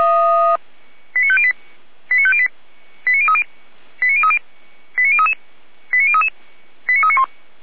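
Five-tone selective call (ZVEI), heard as quick runs of five tones, tones of about 70 ms. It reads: 89508 89508 89030 89030 89033 89033 89391
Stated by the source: ZVEI five-tone selective call 89140 received with an SDR, hosted by Feuerwehr Orlamünde (fire brigade).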